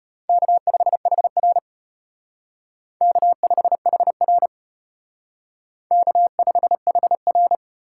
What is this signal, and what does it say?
Morse code, heard as a keyed tone, about 32 wpm, 700 Hz. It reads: K5HR K5HR K5HR